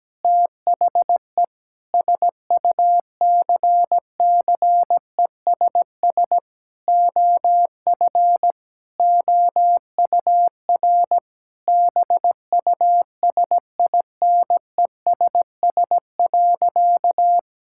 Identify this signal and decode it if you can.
Morse code, heard as a keyed tone, about 17 wpm, 700 Hz. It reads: THE SUCCESS OF OUR BUSINESS.
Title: Sentences from Top 1000 Words - Review Part 6 - 17wpm